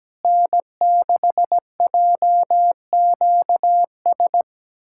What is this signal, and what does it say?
Morse code, heard as a keyed tone, about 17 wpm, 700 Hz. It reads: N6JQS
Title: USA Callsign DE Callsign - 5 Characters - 17wpm